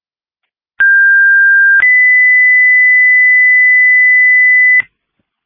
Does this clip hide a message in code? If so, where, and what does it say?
Yes: two-tone page from 0.8 to 4.8 s, heard as a one-second tone, then a three-second tone, about 1600 then 2000 Hz, with noise.